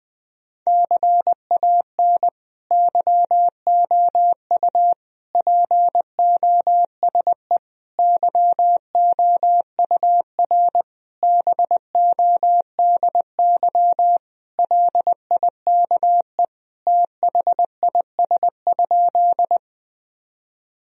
Morse code, heard as a keyed tone, about 21 wpm, 700 Hz. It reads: CAN YOU POSE YOUR BODY LIKE THIS?
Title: Sentences from Top 1000 Words - Review Part 2 - 20wpm